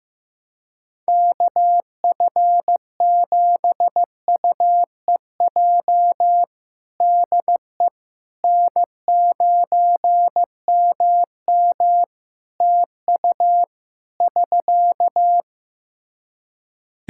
Morse code, heard as a keyed tone, about 15 wpm, 700 Hz. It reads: KF7UEJ DE N9MM TU <SK>
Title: Practice QSO - Exchange 4 - Ending - 15wpm